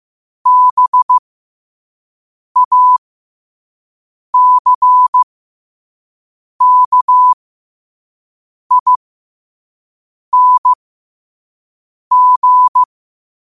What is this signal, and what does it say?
Morse code, heard as a keyed tone, about 15 wpm, spaced as for 5 wpm, 1000 Hz. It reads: BACKING
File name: qotdfiles / 5wpm